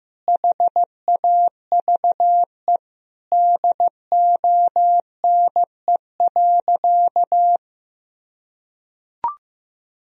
Morse code, heard as a keyed tone, about 15 wpm, 700 Hz. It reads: HAVE DONE.
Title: Sentences from Top 1000 Words - Review Part 3 - 15wpm